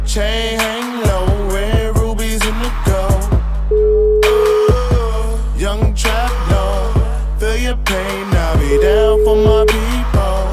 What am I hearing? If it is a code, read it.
Ringback tone (Europe, 425 Hz), with noise.